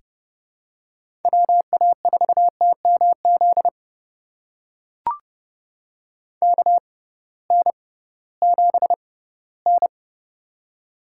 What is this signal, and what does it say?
Morse code, heard as a keyed tone, about 30 wpm, spaced as for 18 wpm, 700 Hz. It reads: WA4TMZ KN7N
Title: USA Call Signs - 6x Character Spacing - 30wpm